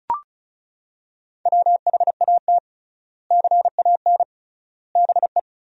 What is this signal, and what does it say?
Morse code, heard as a keyed tone, about 36 wpm, 700 Hz. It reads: WHAT CAN BE
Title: 3x Word-Spacing - Sentences from Top 500 Words - 35wpm